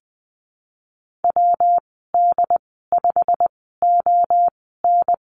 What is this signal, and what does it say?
Morse code, heard as a keyed tone, about 20 wpm, spaced as for 15 wpm, 700 Hz. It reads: WD5ON
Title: 167 Lesson 27g - Number 7 - Call Signs 20wpm